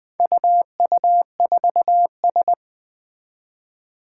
Morse code, heard as a keyed tone, about 20 wpm, 700 Hz. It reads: UU4S
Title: Four Challenging Letter-Number Combinations 20wpm